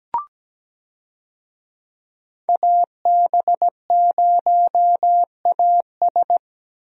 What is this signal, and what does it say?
Morse code, heard as a keyed tone, about 17 wpm, 700 Hz. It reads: AB0AS
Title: USA Call Signs - 5 Characters - 17wpm